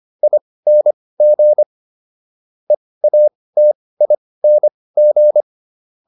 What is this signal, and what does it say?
Morse code, heard as a keyed tone, about 25 wpm, spaced as for 18 wpm, 600 Hz. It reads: ING EATING
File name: ING6 18